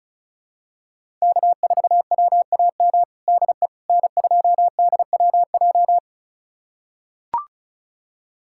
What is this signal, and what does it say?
Morse code, heard as a keyed tone, about 36 wpm, 700 Hz. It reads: K4WAM DE N2DWJ
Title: USA Callsign DE Callsign - 5 Characters - 35wpm